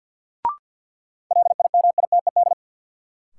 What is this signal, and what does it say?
Morse code, heard as a keyed tone, about 53 wpm, 700 Hz. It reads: LIMITED
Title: Sentences from Top 900 Words - New Word Drill - 50wpm